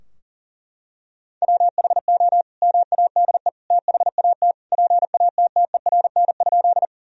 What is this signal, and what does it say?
Morse code, heard as a keyed tone, about 40 wpm, 700 Hz. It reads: WHO MADE THAT PATTERN?
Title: Sentences from Top 1000 Words - Review Part 2 - 40wpm